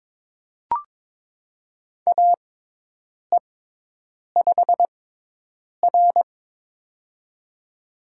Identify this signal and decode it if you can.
Morse code, heard as a keyed tone, about 21 wpm, spaced as for 8 wpm, 700 Hz. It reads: AE5R
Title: USA Call Signs - 4 Characters - 6x Character Spacing - 22wpm